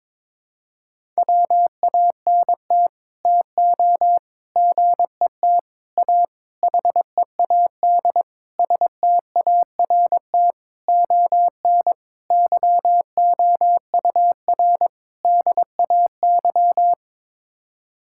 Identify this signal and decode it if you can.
Morse code, heard as a keyed tone, about 22 wpm, 700 Hz. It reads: WANT TO GET A HEAD START ON YOUR DAY
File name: Sets of 9 Words - 22wpm